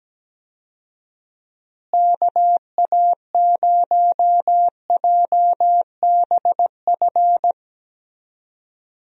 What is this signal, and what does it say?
Morse code, heard as a keyed tone, about 17 wpm, 700 Hz. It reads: KA0JBF
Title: USA Call Signs - 6 Characters - Repeated 3x - 17wpm